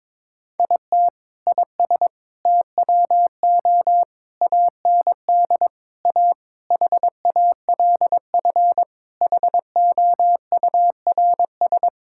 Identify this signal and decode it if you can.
Morse code, heard as a keyed tone, about 22 wpm, 700 Hz. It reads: IT IS TWO AND A HALF HOURS
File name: practice / 22wpm / Sentences with 4-Word Phrases - 22wpm